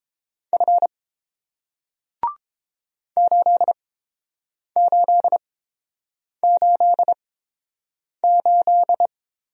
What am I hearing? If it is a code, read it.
Morse code, heard as a keyed tone, about 30 wpm, spaced as for 7 wpm, 700 Hz. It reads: F 8888